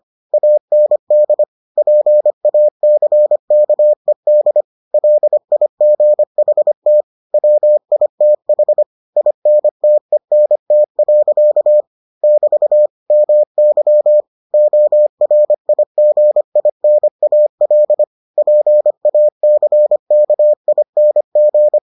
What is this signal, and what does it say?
Morse code, heard as a keyed tone, about 25 wpm, 600 Hz. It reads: AND PACKED LIGHT WITH INTENT. <BT> MY ORIGINAL PACKING